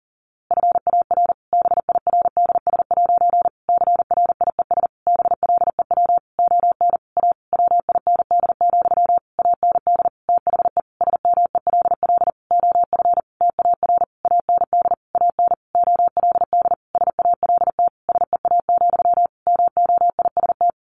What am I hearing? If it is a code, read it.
Morse code, heard as a keyed tone, about 40 wpm, 700 Hz. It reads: FAR BIRDS' CRIES BLEW ON A WIND, AND THE SMELL OF TAR AND AN OLD SALT SEA, MOIST
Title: A Sound Of Thunder 40wpm